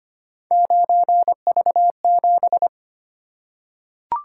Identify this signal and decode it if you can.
Morse code, heard as a keyed tone, about 25 wpm, 700 Hz. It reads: 9V7
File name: Three Letter-Number Combinations - 25wpm